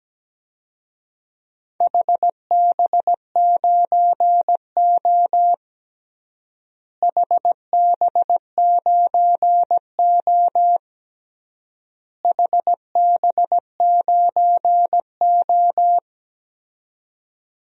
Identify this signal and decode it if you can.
Morse code, heard as a keyed tone, about 17 wpm, 700 Hz. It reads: HB9O HB9O HB9O
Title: DXCC-100 Call Signs - 4 Characters - Repeated 3x - 17wpm